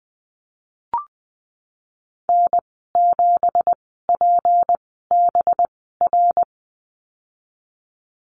Morse code, heard as a keyed tone, about 20 wpm, spaced as for 15 wpm, 700 Hz. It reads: N7PBR
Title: 186 Lesson 30g - Letter b - Call Signs 20wpm